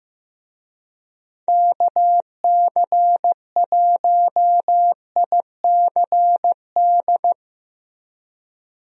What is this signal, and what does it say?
Morse code, heard as a keyed tone, about 15 wpm, 700 Hz. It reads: KC1ICD
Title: USA Call Signs - 6 Characters - Repeated 2x - 15wpm